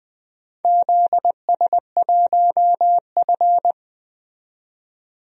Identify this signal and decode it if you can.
Morse code, heard as a keyed tone, about 20 wpm, 700 Hz. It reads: ZS1F
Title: DXCC-100 Call Signs - 4 Characters - 20wpm